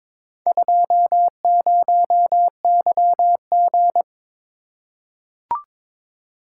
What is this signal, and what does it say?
Morse code, heard as a keyed tone, about 22 wpm, 700 Hz. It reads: 20YG
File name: practice / 22wpm / Four Letter-Number Combinations - 22wpm